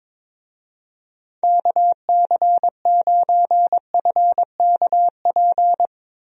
Morse code, heard as a keyed tone, about 22 wpm, 700 Hz. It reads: KC9FKP
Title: USA Call Signs - 6 Characters - Repeated 2x - 22wpm